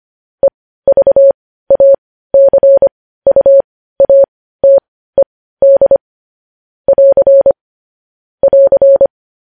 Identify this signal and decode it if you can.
Morse code, heard as a keyed tone, about 25 wpm, spaced as for 15 wpm, 550 Hz. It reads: EVACUATED <AR> <AR>
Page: News Headlines in Morse Code at 15 WPM